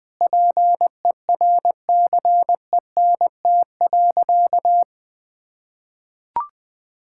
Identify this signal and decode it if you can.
Morse code, heard as a keyed tone, about 21 wpm, 700 Hz. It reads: PERCENT.
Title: Sentences with 8-Word Phrases - 20wpm